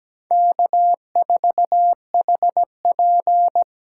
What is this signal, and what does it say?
Morse code, heard as a keyed tone, about 20 wpm, 700 Hz. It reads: K4HP
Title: Speed Racing - USA 1x2 Call Signs 17wpm